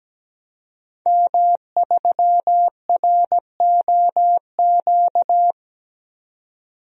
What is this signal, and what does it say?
Morse code, heard as a keyed tone, about 17 wpm, 700 Hz. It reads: M3ROQ
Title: DXCC-100 Call Signs - 5 Characters - 17wpm